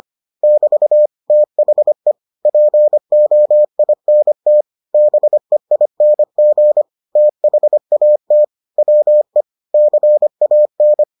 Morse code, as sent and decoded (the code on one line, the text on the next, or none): -...- / - .... / .--. --- .. -. - / -... .. -. --. / - .... .- - / .-- . / -.-. .- -.
<BT> THE POINT BEING THAT WE CAN